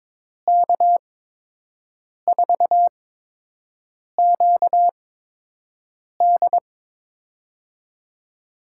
Morse code, heard as a keyed tone, about 22 wpm, spaced as for 6 wpm, 700 Hz. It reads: K4QD
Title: USA Call Signs - 5 Characters - 8x Character Spacing - 22wpm